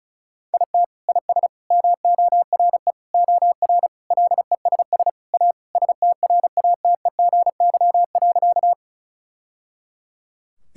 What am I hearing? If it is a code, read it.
Morse code, heard as a keyed tone, about 34 wpm, 700 Hz. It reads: IT IS MORE OR LESS A STRATEGY.
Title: Sentences from Top 1000 Words - Review Part 5 - 35wpm